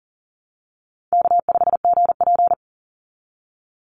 Morse code, heard as a keyed tone, about 40 wpm, 700 Hz. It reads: K5GP
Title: Easy Call Signs Repeated 2x - 40wpm